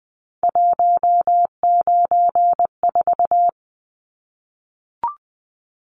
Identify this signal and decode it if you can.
Morse code, heard as a keyed tone, about 20 wpm, 700 Hz. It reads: 194
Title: Numbers - 3-Digit 20wpm